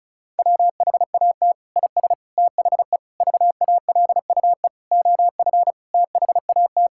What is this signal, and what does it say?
Morse code, heard as a keyed tone, about 36 wpm, 690 Hz. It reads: WHAT IS THE VALUE OF THAT
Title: Sentences from Top 1000 Words - Review Part 6 - 35wpm